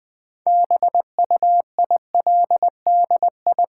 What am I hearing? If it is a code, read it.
Morse code, heard as a keyed tone, about 20 wpm, 700 Hz. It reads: BUILDI
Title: Build A Word - 3 Letters At A Time - 8-Letter Words - 20wpm